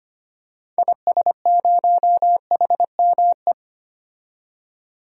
Morse code, heard as a keyed tone, about 25 wpm, 700 Hz. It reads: IS0HME
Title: DXCC-100 Call Signs - 6 Characters - 25wpm